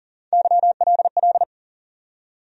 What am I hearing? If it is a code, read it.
Morse code, heard as a keyed tone, about 40 wpm, 700 Hz. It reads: YLL
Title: USA Call Sign Postfixes - 3 Letters 40wpm